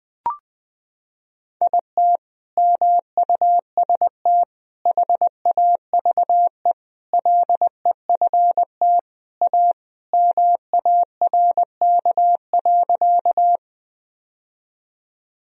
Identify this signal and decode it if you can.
Morse code, heard as a keyed tone, about 20 wpm, 700 Hz. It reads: IT MUST HAVE LEFT A MARK.
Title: Sentences from Top 1000 Words - Review Part 1 - 20wpm